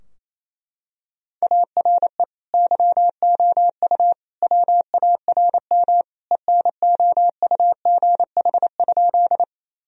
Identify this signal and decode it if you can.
Morse code, heard as a keyed tone, about 27 wpm, 690 Hz. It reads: ARE YOU WARM ENOUGH?
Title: Sentences from Top 1000 Words - Review Part 2 - 28wpm